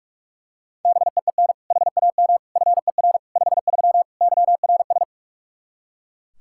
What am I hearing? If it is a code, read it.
Morse code, heard as a keyed tone, about 41 wpm, 700 Hz. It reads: BEEN HAM FER 53 YRS
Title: Practice QSO - Exchange 3 - More Info - 45wpm